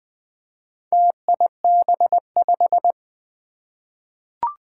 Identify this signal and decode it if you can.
Morse code, heard as a keyed tone, about 20 wpm, 700 Hz. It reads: TIB5
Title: Four Challenging Letter-Number Combinations 20wpm